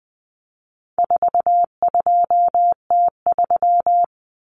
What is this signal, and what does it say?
Morse code, heard as a keyed tone, about 20 wpm, 700 Hz. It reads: 42T3
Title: Common Cut Numbers - 4-Digit 20wpm